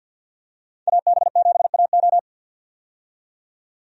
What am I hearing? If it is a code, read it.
Morse code, heard as a keyed tone, about 52 wpm, 700 Hz. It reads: AB7AO
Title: USA Call Signs - 5 Characters - 50wpm